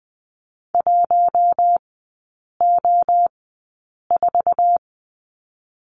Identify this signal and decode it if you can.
Morse code, heard as a keyed tone, about 20 wpm, spaced as for 8 wpm, 700 Hz. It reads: 1O4
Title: 047 Lesson 9e - Number 4 - Three Characters 20wpm